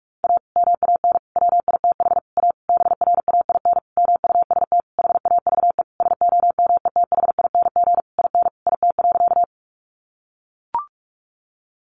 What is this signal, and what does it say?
Morse code, heard as a keyed tone, about 44 wpm, 700 Hz. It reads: A MAN WITH A BRAIN MUST HAVE SOMETHING IN IT.
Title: Sentences from Top 1100 Words 45wpm